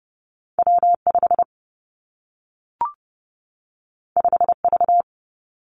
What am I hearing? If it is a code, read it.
Morse code, heard as a keyed tone, about 30 wpm, 700 Hz. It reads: W 5 5 V
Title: Two Challenging Letter-Number Combinations 30wpm